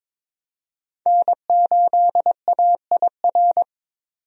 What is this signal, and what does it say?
Morse code, heard as a keyed tone, about 22 wpm, 700 Hz. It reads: N8AIR
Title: USA Call Signs - 5 Characters - 22wpm